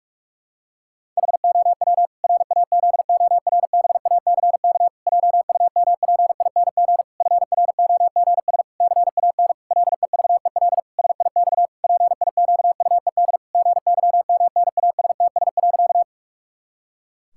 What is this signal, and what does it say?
Morse code, heard as a keyed tone, about 46 wpm, 700 Hz. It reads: HOW RAZORBACK JUMPING FROGS CAN LEVEL SIX PIQUED GYMNASTS.
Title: Pangram Challenge v2 - 45wpm